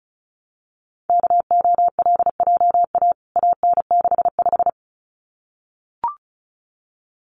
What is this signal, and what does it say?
Morse code, heard as a keyed tone, about 34 wpm, 700 Hz. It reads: KOLJA AN65
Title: CWT Exchange 35wpm